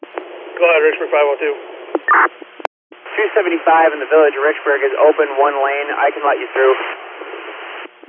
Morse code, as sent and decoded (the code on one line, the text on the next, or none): none